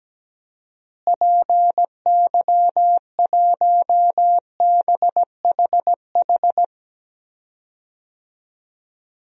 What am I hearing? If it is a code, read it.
Morse code, heard as a keyed tone, about 17 wpm, 700 Hz. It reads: PY1BHH
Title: DXCC-100 Call Signs - 6 Characters - Repeated 2x - 17wpm